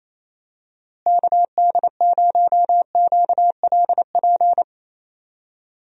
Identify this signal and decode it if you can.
Morse code, heard as a keyed tone, about 28 wpm, 700 Hz. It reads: KD0QLP
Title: USA Call Signs - 6 Characters - Repeated 3x - 28wpm